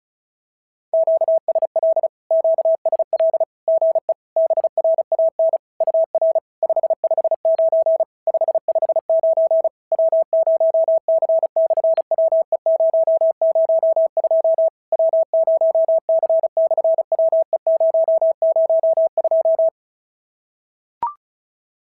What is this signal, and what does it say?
Morse code, heard as a keyed tone, about 36 wpm, 660 Hz. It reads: QSL QSL GE BRAN UR 559 559 W0C/WE002 W0C/WE002
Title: SOTA QSO Practice 35wpm